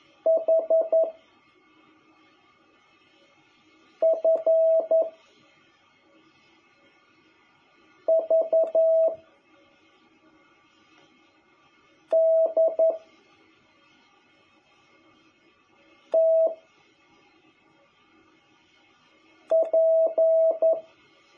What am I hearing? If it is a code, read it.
Morse code, heard as a keyed tone, about 11 wpm, 660 Hz. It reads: H F V D T P
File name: jh8cbh / 2021cw